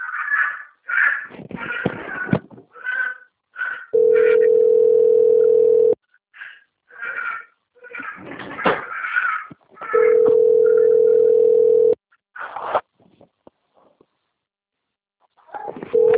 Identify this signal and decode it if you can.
Ringback tone (North America), 440 + 480 Hz.